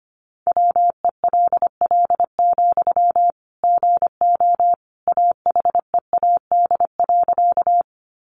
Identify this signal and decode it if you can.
Morse code, heard as a keyed tone, about 25 wpm, 700 Hz. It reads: WELL, GO AHEAD.
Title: Sentences from Top 1100 Words 25wpm